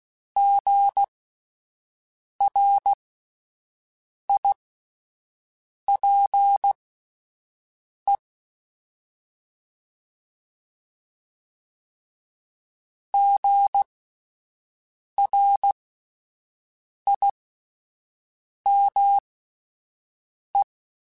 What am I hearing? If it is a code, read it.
Morse code, heard as a keyed tone, about 16 wpm, spaced as for 5 wpm, 800 Hz. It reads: GRIPE GRIME
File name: Spaced Alphas / Spaced Alphas mp3s / Gs sp @10